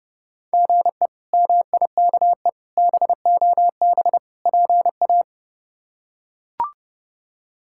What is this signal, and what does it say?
Morse code, heard as a keyed tone, about 32 wpm, 700 Hz. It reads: GE MIKE BOB PA
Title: K1USN SST QSO Practice - 30wpm